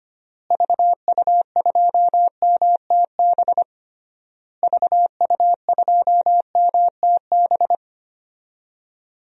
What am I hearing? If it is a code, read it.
Morse code, heard as a keyed tone, about 25 wpm, 700 Hz. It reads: VU2MTB VU2MTB
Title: DXCC-100 Call Signs - 6 Characters - Repeated 3x - 25wpm